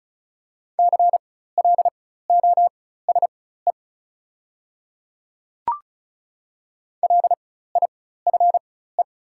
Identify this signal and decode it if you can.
Morse code, heard as a keyed tone, about 36 wpm, spaced as for 16 wpm, 690 Hz. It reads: CLOSE LIFE
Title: 4x Character Spacing - Top 300 Words - 35wpm